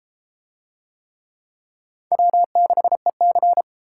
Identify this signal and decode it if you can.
Morse code, heard as a keyed tone, about 30 wpm, 700 Hz. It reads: W6EC